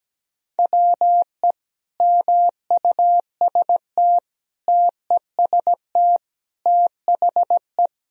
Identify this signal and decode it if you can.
Morse code, heard as a keyed tone, about 17 wpm, 700 Hz. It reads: WE MUST TEST THE
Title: Sentences from Top 1000 Words - Review Part 3 - 17wpm